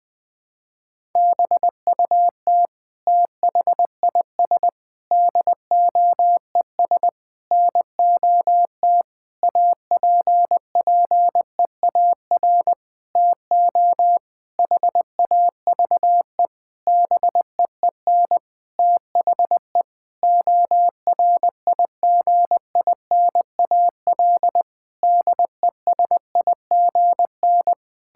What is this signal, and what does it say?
Morse code, heard as a keyed tone, about 20 wpm, 700 Hz. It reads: BUT THIS DOES NOT APPEAR TO HAVE BEEN THE ORIGINAL DESIGN
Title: Sentences from Top 1000 Words - Review Part 6 - 20wpm